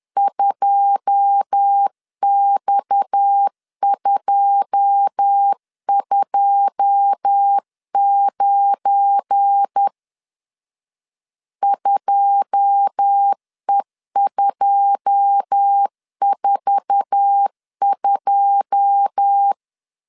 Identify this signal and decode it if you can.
Morse code, heard as a keyed tone, about 10 wpm, 800 Hz. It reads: 2X229 2E242
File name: LEZIONI / 9 2 10wpm 2736